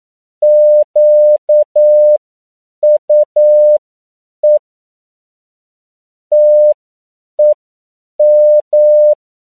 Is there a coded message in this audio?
Yes: Morse code from 0.4 to 9.1 s, heard as a keyed tone, about 9 wpm, spaced as for 7 wpm, 600 Hz. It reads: QUE TEM